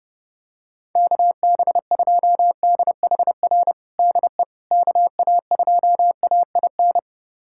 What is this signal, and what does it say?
Morse code, heard as a keyed tone, about 30 wpm, 700 Hz. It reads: KB2DHR DE KA2AIN
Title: USA Callsign DE Callsign - 6 Characters - 30wpm